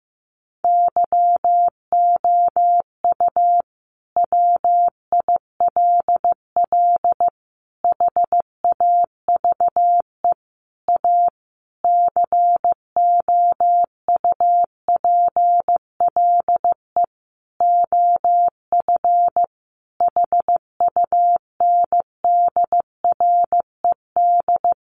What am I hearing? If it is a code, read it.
Morse code, heard as a keyed tone, about 15 wpm, 700 Hz. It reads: YOU WILL HAVE A COUPLE OF HUNDRED
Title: Sentences from Top 1100 Words 15wpm